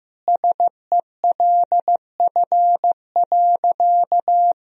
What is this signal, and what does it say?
Morse code, heard as a keyed tone, about 15 wpm, 700 Hz. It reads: SELF.
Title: Sentences from Top 1000 Words - Review Part 1 - 15wpm